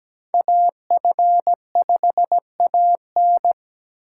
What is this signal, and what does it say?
Morse code, heard as a keyed tone, about 17 wpm, 700 Hz. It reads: AF5AN
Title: USA Call Signs - 5 Characters - 2x Character Spacing - 17wpm